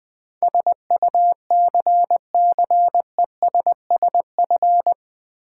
Morse code, heard as a keyed tone, about 20 wpm, 700 Hz. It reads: SUCCESSF
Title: Build A Word - 4 Letters At A Time - 12-Letter Words - 20wpm